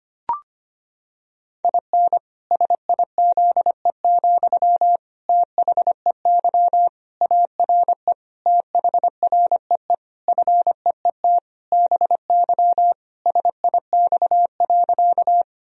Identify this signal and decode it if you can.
Morse code, heard as a keyed tone, about 25 wpm, 700 Hz. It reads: IN SIZE, THEY ARE THREE FEET BY SIX.
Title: Sentences from Top 1000 Words - Review Part 2 - 25wpm